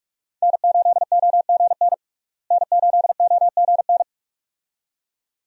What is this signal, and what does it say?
Morse code, heard as a keyed tone, about 42 wpm, 700 Hz. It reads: N8OGN N8OGN
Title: USA Call Signs - 5 Characters - Repeated 2x - 45wpm